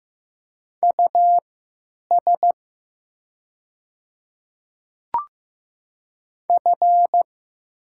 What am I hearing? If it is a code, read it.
Morse code, heard as a keyed tone, about 15 wpm, spaced as for 9 wpm, 700 Hz. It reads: US F
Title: 3x Character Spacing - Top 300 Words - 15wpm